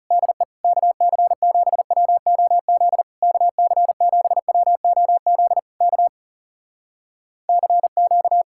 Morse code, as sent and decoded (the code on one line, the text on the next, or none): -.. / -.- -.-. --... .-- --- --.. / -.- -.-. --... .-- --- --.. / -.- / -.-. --.-
DE KC7WOZ KC7WOZ K CQ